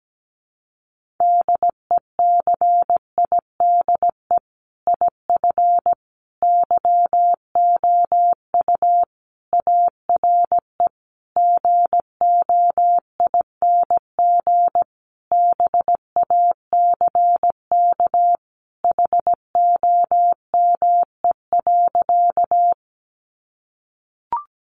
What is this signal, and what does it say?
Morse code, heard as a keyed tone, about 17 wpm, 700 Hz. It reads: DECIDE IF YOU ARE GOING BACK HOME.
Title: Sentences from Top 500 Words 17wpm